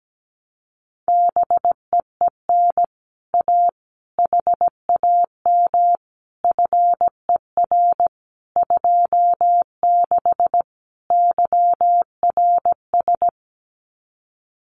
Morse code, heard as a keyed tone, about 17 wpm, 700 Hz. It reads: BEEN A HAM FER 26 YRS
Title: Common QSO Phrases Encore 17wpm